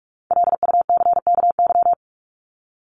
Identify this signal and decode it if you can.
Morse code, heard as a keyed tone, about 44 wpm, 700 Hz. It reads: LUCKY
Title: Top 1751-2000 Words 45wpm